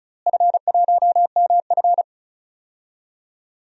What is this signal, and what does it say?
Morse code, heard as a keyed tone, about 36 wpm, 700 Hz. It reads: F1MF
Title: DXCC-100 Call Signs - 4 Characters - Repeated 3x - 35wpm